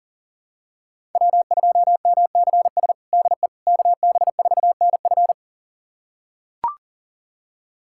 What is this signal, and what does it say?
Morse code, heard as a keyed tone, about 40 wpm, 700 Hz. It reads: W2MCS DE KB4NF
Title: USA Callsign DE Callsign - 5 Characters - 40wpm